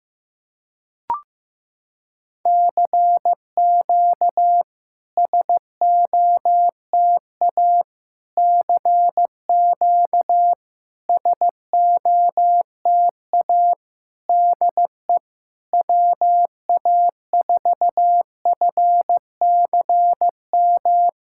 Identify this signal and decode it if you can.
Morse code, heard as a keyed tone, about 15 wpm, 700 Hz. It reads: CQ SOTA CQ SOTA DE WA4FCM
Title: SOTA QSO Practice 15wpm